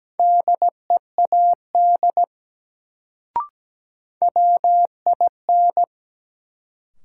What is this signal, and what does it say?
Morse code, heard as a keyed tone, about 17 wpm, 700 Hz. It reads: DEAD WIN
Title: Sentences from Top 1500 Words - New Word Drill Alt - 17wpm